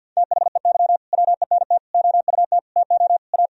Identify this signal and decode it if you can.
Morse code, heard as a keyed tone, about 53 wpm, 700 Hz. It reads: THEY WENT OUT TO A
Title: Sentences from Top 1000 Words - Review Part 2 - 50wpm